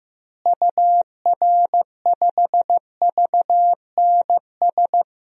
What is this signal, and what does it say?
Morse code, heard as a keyed tone, about 15 wpm, 700 Hz. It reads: UR5VNS
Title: DXCC-100 Call Signs - 6 Characters - Repeated 2x - 15wpm